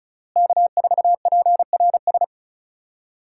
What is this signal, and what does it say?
Morse code, heard as a keyed tone, about 36 wpm, 700 Hz. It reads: K4PRS
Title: USA Call Signs - 5 Characters - Repeated 3x - 35wpm